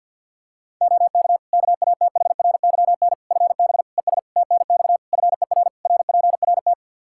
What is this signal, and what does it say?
Morse code, heard as a keyed tone, about 50 wpm, 700 Hz. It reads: OK KATHRYN FB ES TNX FER RPRT
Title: Practice QSO - Exchange 2 - Info - 50wpm